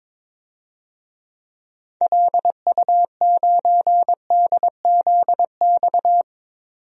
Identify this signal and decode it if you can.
Morse code, heard as a keyed tone, about 22 wpm, 700 Hz. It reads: LU9DZX
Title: DXCC-100 Call Signs - 6 Characters - 22wpm